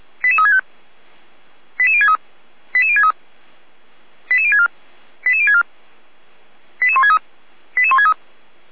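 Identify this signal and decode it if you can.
Five-tone selective call (ZVEI), heard as quick runs of five tones, tones of about 70 ms. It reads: 88365 88063 88063 88064 88064 88163 88163
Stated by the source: Five-tone ZVEI selective call received on SDR.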